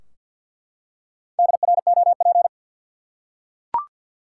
Morse code, heard as a keyed tone, about 51 wpm, 700 Hz. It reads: DROP
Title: Four Letter Words - 50wpm